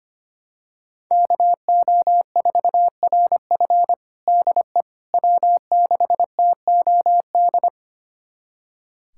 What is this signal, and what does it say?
Morse code, heard as a keyed tone, about 25 wpm, 700 Hz. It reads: KO4RF DE W6TOD
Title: USA Callsign DE Callsign - 5 Characters - 25wpm